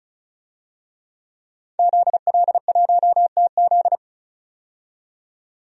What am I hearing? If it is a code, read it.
Morse code, heard as a keyed tone, about 34 wpm, 700 Hz. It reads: ZL1TZ
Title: DXCC-100 Call Signs - 5 Characters - 35wpm